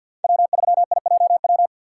Morse code, heard as a keyed tone, about 48 wpm, 700 Hz. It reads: W3IJW
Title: USA Call Signs - 5 Characters - 6x Character Spacing - 50wpm